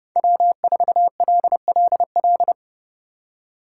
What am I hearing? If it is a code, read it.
Morse code, heard as a keyed tone, about 30 wpm, 700 Hz. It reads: W4LLL
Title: USA Call Signs - 5 Characters - 3x Character Spacing - 30wpm